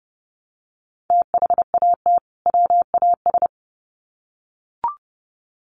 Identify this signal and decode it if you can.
Morse code, heard as a keyed tone, about 28 wpm, 700 Hz. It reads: THAT WAS
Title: Sets of 2 Words 30wpm